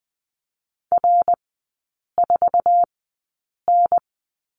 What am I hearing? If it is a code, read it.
Morse code, heard as a keyed tone, about 20 wpm, spaced as for 8 wpm, 700 Hz. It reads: R4N
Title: 107 Lesson 18e - Letter m - Three Characters 20wpm